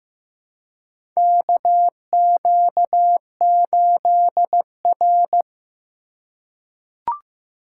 Morse code, heard as a keyed tone, about 15 wpm, 700 Hz. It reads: KQ8R
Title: USA Call Signs - 4 Characters - 4x Character Spacing - 15wpm